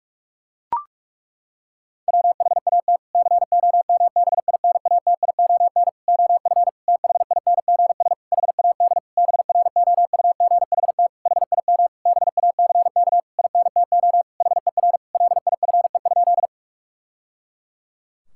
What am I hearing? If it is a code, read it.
Morse code, heard as a keyed tone, about 41 wpm, 700 Hz. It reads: WHAT COMBINATION OF THINGS HAD BROUGHT HIM BACK INTO HER LIFE?
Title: Sentences from Top 1000 Words - Review Part 6 - 45wpm